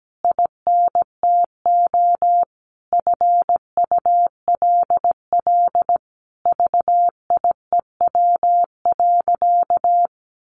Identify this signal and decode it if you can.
Morse code, heard as a keyed tone, about 17 wpm, 700 Hz. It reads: INTO FULL VIEW.
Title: Sentences from Top 1100 Words 17wpm